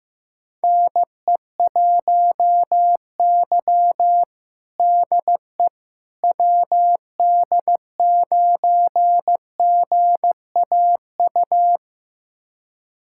Morse code, heard as a keyed tone, about 15 wpm, 700 Hz. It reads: NE1Y DE WD9GAU
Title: USA Callsign DE Callsign 15wpm